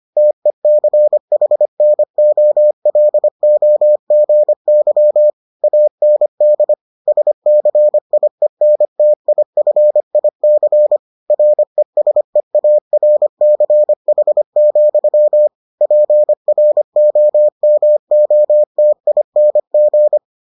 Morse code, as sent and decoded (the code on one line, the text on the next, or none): - . -.-. .... -. --- .-.. --- --. -.-- / .- -. -.. / ... -.-. .. -. - .. ..-. .. -.-. / .-. ... .- .-. -.-. .... --..-- / .--. .-. --- -- --- - .. -. --.
TECHNOLOGY AND SCIENTIFIC RESEARCH, PROMOTING